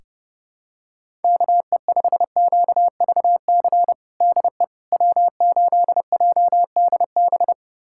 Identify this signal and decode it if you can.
Morse code, heard as a keyed tone, about 30 wpm, 700 Hz. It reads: KE5QVC DE W8JDB
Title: USA Callsign DE Callsign 30wpm